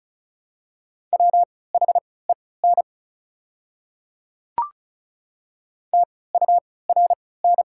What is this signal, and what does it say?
Morse code, heard as a keyed tone, about 36 wpm, spaced as for 20 wpm, 700 Hz. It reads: WHEN TURN
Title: 3x Character Spacing - Top 200 Words - 35wpm